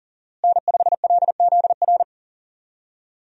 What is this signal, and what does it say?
Morse code, heard as a keyed tone, about 40 wpm, 700 Hz. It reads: N5LZR
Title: USA Call Signs - 5 Characters - Repeated 3x - 40wpm